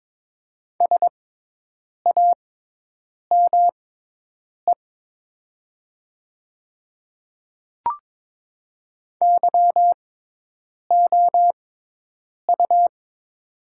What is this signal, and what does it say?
Morse code, heard as a keyed tone, about 22 wpm, spaced as for 8 wpm, 700 Hz. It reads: SAME YOU
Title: 6x Character Spacing - Top 200 Words - 22wpm